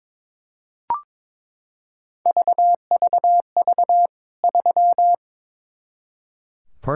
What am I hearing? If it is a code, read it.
Morse code, heard as a keyed tone, about 22 wpm, 700 Hz. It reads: VVV 3
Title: K1USN SST QSO Practice 22wpm